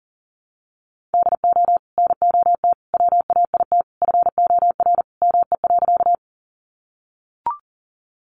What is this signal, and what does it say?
Morse code, heard as a keyed tone, about 40 wpm, 700 Hz. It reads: DO NOT WAIT FOR ME.
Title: Sentences from Top 600 Words 40wpm